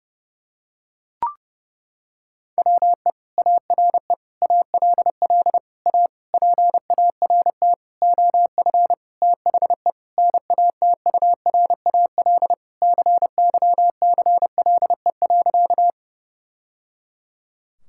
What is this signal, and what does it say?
Morse code, heard as a keyed tone, about 30 wpm, 700 Hz. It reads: WE ARE ALL A PART OF THE NATURAL CYCLE.
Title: Sentences from Top 1000 Words - Review Part 6 - 30wpm